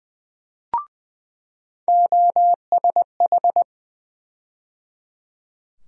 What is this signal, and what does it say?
Morse code, heard as a keyed tone, about 21 wpm, 700 Hz. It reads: OSH
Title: USA Call Sign Postfixes - 3 Letters 20wpm